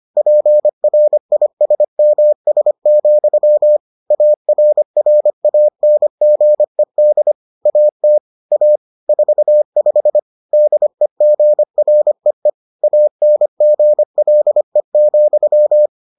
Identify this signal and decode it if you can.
Morse code, heard as a keyed tone, about 25 wpm, 600 Hz. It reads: PRISMS, ARRANGED AT A 45 DEGREE ANGLE,